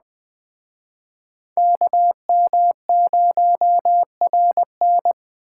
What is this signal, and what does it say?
Morse code, heard as a keyed tone, about 21 wpm, 700 Hz. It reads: KM0RN